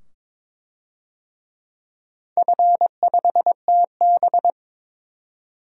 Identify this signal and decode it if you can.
Morse code, heard as a keyed tone, about 22 wpm, 700 Hz. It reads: F5TB
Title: DXCC-100 Call Signs - 4 Characters - 22wpm